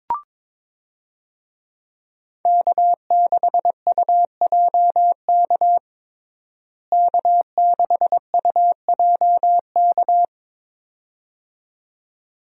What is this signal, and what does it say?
Morse code, heard as a keyed tone, about 22 wpm, 700 Hz. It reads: K6UJK K6UJK
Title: USA Call Signs - 5 Characters - Repeated 2x - 22wpm